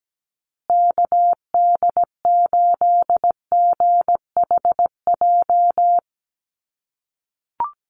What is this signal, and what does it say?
Morse code, heard as a keyed tone, about 17 wpm, 700 Hz. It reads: KD8GHJ
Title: USA Call Signs Repeated 2x 17wpm